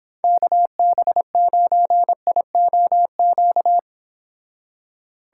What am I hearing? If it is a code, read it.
Morse code, heard as a keyed tone, about 31 wpm, 700 Hz. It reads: KB9IOQ